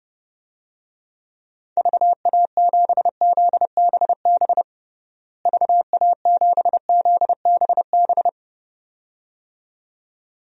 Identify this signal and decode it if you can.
Morse code, heard as a keyed tone, about 32 wpm, 700 Hz. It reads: VA7ZBB VA7ZBB
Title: DXCC-100 Call Signs - 6 Characters - Repeated 2x - 30wpm